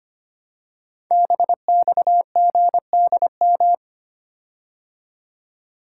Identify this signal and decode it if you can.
Morse code, heard as a keyed tone, about 25 wpm, 700 Hz. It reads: BXGDM
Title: Code Groups - 5 Letters - 25wpm